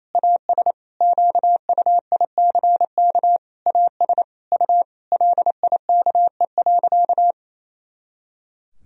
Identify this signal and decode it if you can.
Morse code, heard as a keyed tone, about 28 wpm, 700 Hz. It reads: AS QUICK AS U LIKE.
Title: Sentences from Top 500 Words and Abbreviations 28wpm